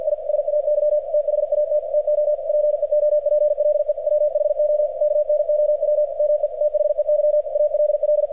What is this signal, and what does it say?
Morse code, heard as a keyed tone, about 48 wpm, 600 Hz, with radio noise. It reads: I SAY THAT TO PEOPLE WHO KNOW ME THEY ALL
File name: archived / CRSnet